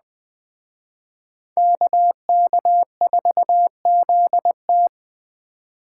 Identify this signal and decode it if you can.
Morse code, heard as a keyed tone, about 19 wpm, 700 Hz. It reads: KK4ZT